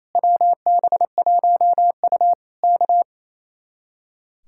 Morse code, heard as a keyed tone, about 27 wpm, 700 Hz. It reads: WB1U K
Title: Speed-Racing Common QSO Phrases - 28wpm